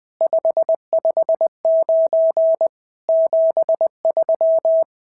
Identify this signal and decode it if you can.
Morse code, heard as a keyed tone, about 20 wpm, 650 Hz. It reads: 559 73
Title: SOTA QSO Practice 20wpm